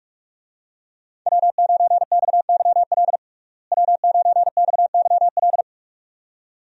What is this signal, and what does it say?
Morse code, heard as a keyed tone, about 42 wpm, 700 Hz. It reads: W9XYL W9XYL
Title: USA Call Signs - 5 Characters - Repeated 2x - 45wpm